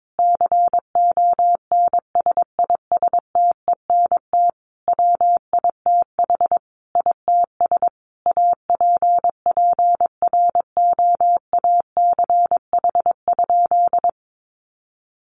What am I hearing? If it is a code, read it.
Morse code, heard as a keyed tone, about 22 wpm, 700 Hz. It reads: CONSISTENT WITH ITS APPROACH?
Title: Sentences from Top 1500 Words 22wpm